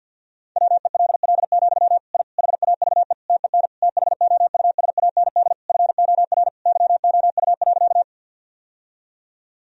Morse code, heard as a keyed tone, about 51 wpm, 700 Hz. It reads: WELL, I HAVE TEN THOUSAND FOR YOU.
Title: Sentences from Top 1000 Words - Review Part 3 - 50wpm